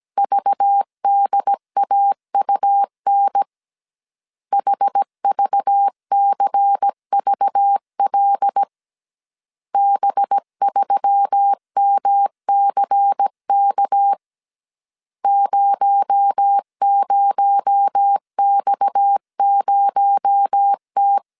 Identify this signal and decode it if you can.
Morse code, heard as a keyed tone, about 17 wpm, 800 Hz. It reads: VDAUN HVCVL B3MCK 00X0T
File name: CW / DET / 1 6 1500C S A 16wpm 2635